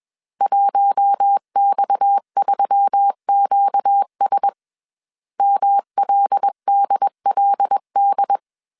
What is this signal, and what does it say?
Morse code, heard as a keyed tone, about 21 wpm, 800 Hz. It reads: 1X3QS MLDLD
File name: CW / DET / 1 6 1500C S A 20wpm 2048